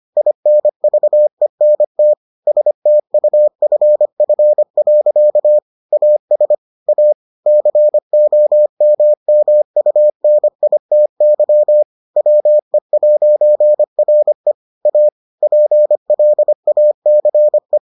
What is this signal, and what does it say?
Morse code, heard as a keyed tone, about 25 wpm, 600 Hz. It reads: INVENT STUFF. AS A COMMUNITY WE'RE A PLACE